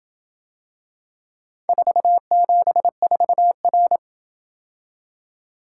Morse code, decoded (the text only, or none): 474R